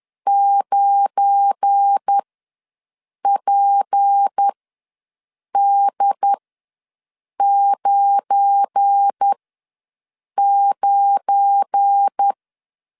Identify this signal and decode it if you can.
Morse code, heard as a keyed tone, about 10 wpm, spaced as for 6 wpm, 800 Hz. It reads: 9PD99
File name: LEZIONI / 7 GF9Z / 7 GF9 ws 10wpm 4034